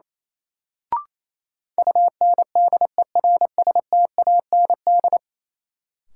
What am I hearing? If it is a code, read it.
Morse code, heard as a keyed tone, about 28 wpm, 700 Hz. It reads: UNDERSTAND
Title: Sentences from Top 600 Words - New Word Drill - 28wpm